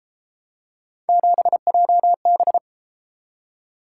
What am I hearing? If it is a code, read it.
Morse code, heard as a keyed tone, about 31 wpm, 700 Hz. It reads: 7JB